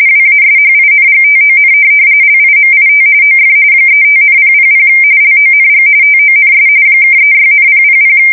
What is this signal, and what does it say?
RTTY, about 45.45 baud. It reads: WASHINGTON LIMIT OF 35,000 LONG TONS (36,000
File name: fsk170 / USS-Massachusetts